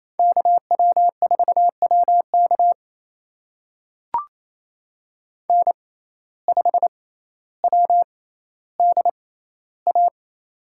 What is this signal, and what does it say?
Morse code, heard as a keyed tone, about 28 wpm, spaced as for 14 wpm, 700 Hz. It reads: KW4WK N5WDA
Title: USA Call Signs - 6x Character Spacing - 28wpm